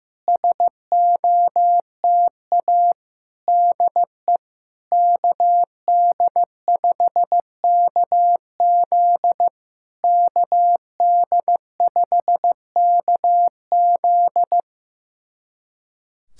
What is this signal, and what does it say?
Morse code, heard as a keyed tone, about 16 wpm, 700 Hz. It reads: SOTA DE KD5KZ KD5KZ
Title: SOTA QSO Practice 15wpm